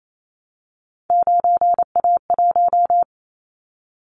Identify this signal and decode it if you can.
Morse code, heard as a keyed tone, about 28 wpm, 700 Hz. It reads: 9A1
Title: Callsign Prefixes for Top 25 DXCC - Repeated 3x - 28wpm